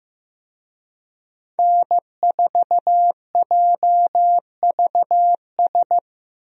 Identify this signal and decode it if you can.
Morse code, heard as a keyed tone, about 15 wpm, 700 Hz. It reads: N4JVS
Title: USA Call Signs - 5 Characters - Repeated 3x - 15wpm